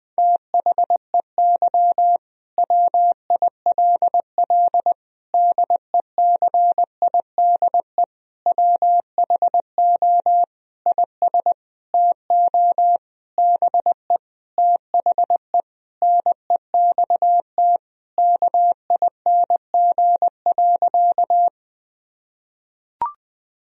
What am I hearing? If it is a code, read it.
Morse code, heard as a keyed tone, about 20 wpm, 700 Hz. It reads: THEY WILL DECIDE WHO IS TO BE THE NEXT KING.
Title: Sentences from Top 1000 Words - Review Part 3 - 20wpm